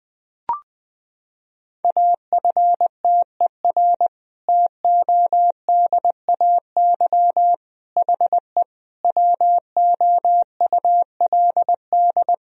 Morse code, heard as a keyed tone, about 21 wpm, 700 Hz. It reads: AFTER TODAY HE WOULD
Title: Sentences from Top 1000 Words - Review Part 4 - 20wpm